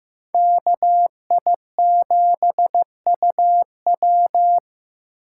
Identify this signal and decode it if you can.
Morse code, heard as a keyed tone, about 15 wpm, 700 Hz. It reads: KI7UW
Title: USA Call Signs - 5 Characters - 8x Character Spacing - 15wpm